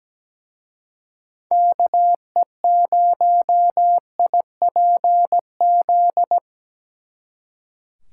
Morse code, heard as a keyed tone, about 20 wpm, 700 Hz. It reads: KE0IPZ
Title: Speed Racing - USA 2x3 Call Signs 17wpm